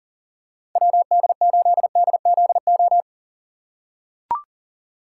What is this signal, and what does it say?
Morse code, heard as a keyed tone, about 40 wpm, 700 Hz. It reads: WD8DZO
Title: USA Call Signs - 6 Characters - 2x Character Spacing - 40wpm